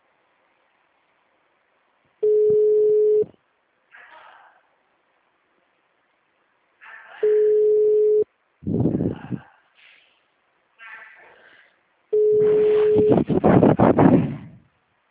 Ringback tone (Europe, 425 Hz).